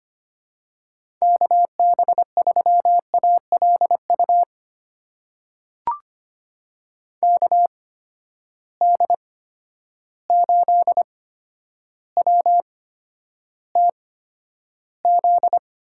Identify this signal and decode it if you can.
Morse code, heard as a keyed tone, about 25 wpm, spaced as for 11 wpm, 700 Hz. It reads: KB3ALU KD8WTZ
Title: USA Call Signs - 6 Characters - 8x Character Spacing - 25wpm